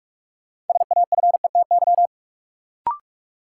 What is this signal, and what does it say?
Morse code, heard as a keyed tone, about 43 wpm, 700 Hz. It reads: SAFETY
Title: Sentences from Top 700 Words - New Word Drill - 45wpm